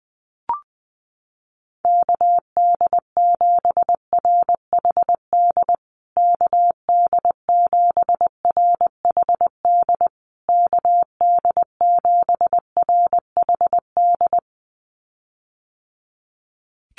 Morse code, heard as a keyed tone, about 20 wpm, 700 Hz. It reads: KD7RHD KD7RHD KD7RHD
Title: USA Call Signs Repeated 3x 20wpm